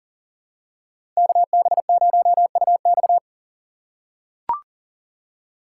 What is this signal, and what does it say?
Morse code, heard as a keyed tone, about 37 wpm, 700 Hz. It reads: KB0UX
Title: USA Call Signs - 5 Characters - 4x Character Spacing - 40wpm